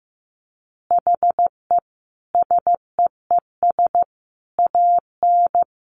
Morse code, heard as a keyed tone, about 15 wpm, 700 Hz. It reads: HE SEES AN
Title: Easy Phrases 15wpm